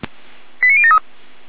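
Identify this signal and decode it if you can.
Five-tone selective call (ZVEI), heard as quick runs of five tones, tones of about 70 ms. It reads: 89073
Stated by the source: ZVEI five-tone selective call received by SDR, from the fire brigade (Feuerwehr) Orlamünde server.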